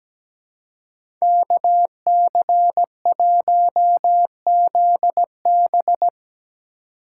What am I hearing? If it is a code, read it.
Morse code, heard as a keyed tone, about 17 wpm, 700 Hz. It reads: KC1ZB
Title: USA Call Signs - 4x Character Spacing - 17wpm